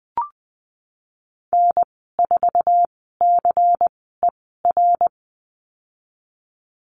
Morse code, heard as a keyed tone, about 20 wpm, spaced as for 15 wpm, 700 Hz. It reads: N4CER
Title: 128 Lesson 21g - Number 6 - Call Signs 20wpm